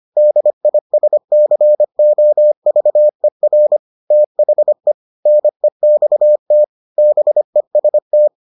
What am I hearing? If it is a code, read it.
Morse code, heard as a keyed tone, about 25 wpm, 600 Hz. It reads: DISCOVER THE NEXT BEST